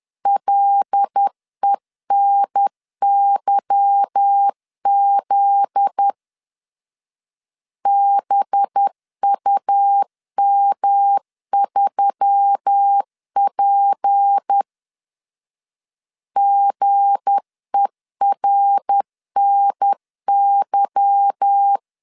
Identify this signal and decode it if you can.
Morse code, heard as a keyed tone, about 10 wpm, 800 Hz. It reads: LENYZ BUM3P GERNY
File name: LEZIONI / 8 YJ45 / 8 Y 10wpm 2615